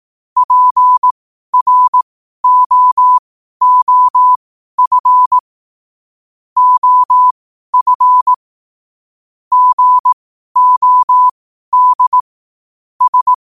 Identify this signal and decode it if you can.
Morse code, heard as a keyed tone, about 18 wpm, spaced as for 12 wpm, 1000 Hz. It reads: PROOF OF GOD S
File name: qotdfiles / 12wpm